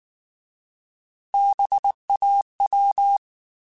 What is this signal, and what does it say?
Morse code, heard as a keyed tone, about 19 wpm, 800 Hz. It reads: BAW
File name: Alphabetical / Alphabetical mp3s / Bs @18